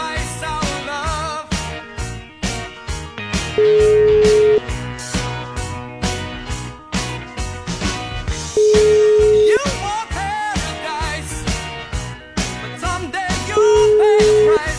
Ringback tone (Europe, 425 Hz), with noise.